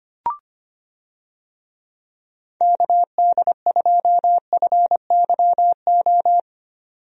Morse code, heard as a keyed tone, about 25 wpm, 700 Hz. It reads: KD2FYO
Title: USA Call Signs - 6 Characters - 25wpm